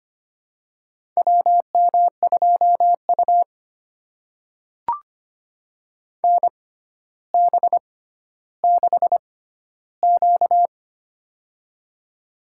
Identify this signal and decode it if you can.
Morse code, heard as a keyed tone, about 25 wpm, spaced as for 13 wpm, 700 Hz. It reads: WM2U NB6Q
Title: USA Call Signs - 4 Characters - 6x Character Spacing - 25wpm